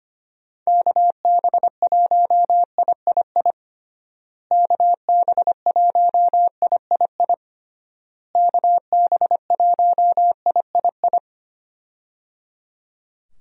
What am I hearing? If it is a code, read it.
Morse code, heard as a keyed tone, about 25 wpm, 700 Hz. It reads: KB1III KB1III KB1III
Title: USA Call Signs - 6 Characters - Repeated 3x - 25wpm